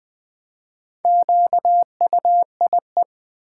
Morse code, heard as a keyed tone, about 20 wpm, 700 Hz. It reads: QUIE
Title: Build A Word - 1 Letter At A Time - 5-Letter Words - 20wpm